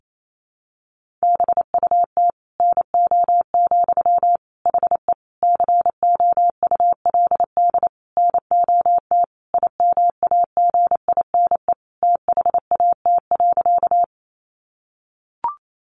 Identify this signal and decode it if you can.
Morse code, heard as a keyed tone, about 28 wpm, 700 Hz. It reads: BUT NO, HE COULD NOT IMAGINE THAT.
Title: Sentences from Top 1400 Words 28wpm